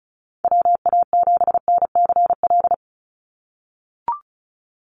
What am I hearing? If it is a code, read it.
Morse code, heard as a keyed tone, about 34 wpm, 700 Hz. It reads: WA7NCL
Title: NA SOTA Activator Callsigns 35wpm